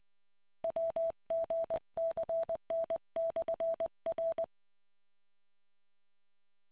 Morse code, heard as a keyed tone, about 20 wpm, 660 Hz. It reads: WGCN/R